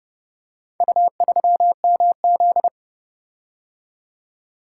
Morse code, heard as a keyed tone, about 29 wpm, 700 Hz. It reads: U3MZ